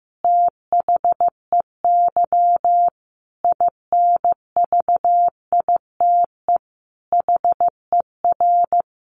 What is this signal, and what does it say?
Morse code, heard as a keyed tone, about 15 wpm, 700 Hz. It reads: THEY INVITE HER
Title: Easy Phrases 15wpm